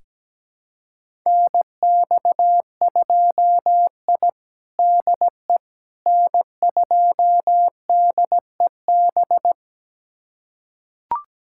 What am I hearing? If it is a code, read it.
Morse code, heard as a keyed tone, about 17 wpm, 700 Hz. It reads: NX2I DE N2DEB
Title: USA Callsign DE Callsign 17wpm